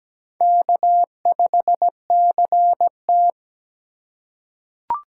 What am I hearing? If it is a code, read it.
Morse code, heard as a keyed tone, about 17 wpm, 700 Hz. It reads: K5CT
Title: USA Call Signs - 4 Characters - 4x Character Spacing - 17wpm